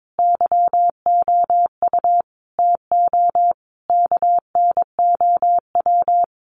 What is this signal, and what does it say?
Morse code, heard as a keyed tone, about 22 wpm, 700 Hz. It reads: YOU TO KNOW
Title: Sets of 5 Words Encore 22wpm